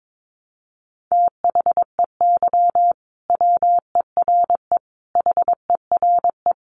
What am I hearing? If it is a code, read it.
Morse code, heard as a keyed tone, about 22 wpm, 700 Hz. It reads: THEY WERE HERE
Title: Sentences from Top 1200 Words 22wpm